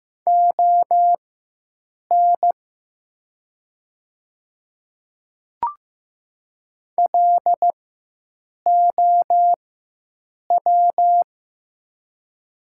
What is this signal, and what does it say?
Morse code, heard as a keyed tone, about 15 wpm, spaced as for 7 wpm, 700 Hz. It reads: ON LOW
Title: 4x Character Spacing - Top 300 Words - 15wpm